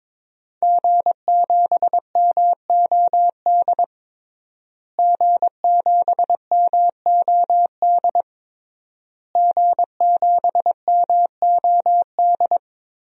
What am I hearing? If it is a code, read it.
Morse code, heard as a keyed tone, about 22 wpm, 700 Hz. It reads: G7MOD G7MOD G7MOD
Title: DXCC-100 Call Signs - 5 Characters - Repeated 3x - 22wpm